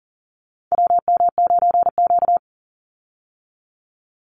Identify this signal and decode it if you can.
Morse code, heard as a keyed tone, about 40 wpm, 700 Hz. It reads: WM9Q
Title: Call Signs Repeated 3x - 40wpm